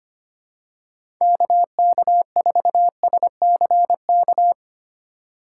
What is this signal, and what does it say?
Morse code, heard as a keyed tone, about 25 wpm, 700 Hz. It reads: KK4SCK
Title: USA Call Signs - 6 Characters - 4x Character Spacing - 25wpm